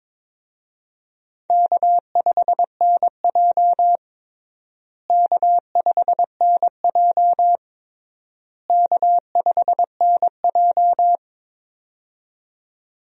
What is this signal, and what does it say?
Morse code, heard as a keyed tone, about 22 wpm, 700 Hz. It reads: K5NJ K5NJ K5NJ
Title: USA Call Signs - 4 Characters - Repeated 3x - 22wpm